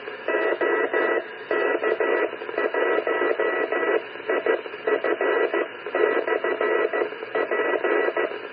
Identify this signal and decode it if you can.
Morse code, heard as a keyed tone, about 14 wpm, 1950 Hz, with noise. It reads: OK1IF/P